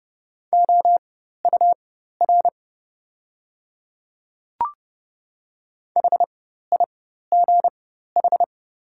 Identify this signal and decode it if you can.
Morse code, heard as a keyed tone, about 30 wpm, spaced as for 14 wpm, 700 Hz. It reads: OUR HIGH
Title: 4x Character Spacing - Top 300 Words - 30wpm